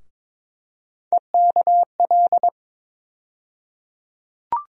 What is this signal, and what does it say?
Morse code, heard as a keyed tone, about 22 wpm, 700 Hz. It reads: EKL
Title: USA Call Sign Postfixes - 3 Letters 22wpm